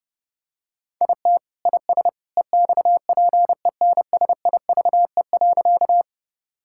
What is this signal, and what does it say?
Morse code, heard as a keyed tone, about 30 wpm, 700 Hz. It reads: IT IS EXPENSIVE.
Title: Sentences from Top 1000 Words - Review Part 6 - 30wpm